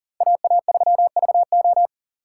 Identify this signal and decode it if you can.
Morse code, heard as a keyed tone, about 48 wpm, 700 Hz. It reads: AA3VO
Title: Speed Racing - USA 2x2 Call Signs 40wpm